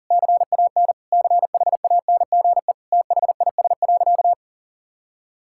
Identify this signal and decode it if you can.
Morse code, heard as a keyed tone, about 40 wpm, 700 Hz. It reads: CAN CHANGE THIS.
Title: Sentences from Top 1000 Words - Review Part 6 - 40wpm